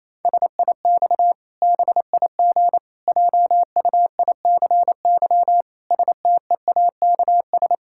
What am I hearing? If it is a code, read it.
Morse code, heard as a keyed tone, about 28 wpm, 700 Hz. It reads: SIX BIG JUICY STEAKS
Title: Pangram Challenge v2 - 28wpm